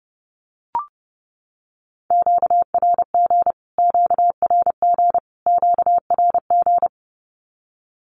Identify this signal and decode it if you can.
Morse code, heard as a keyed tone, about 30 wpm, 700 Hz. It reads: QRG QRG QRG
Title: Q-Codes Repeated 3x 30wpm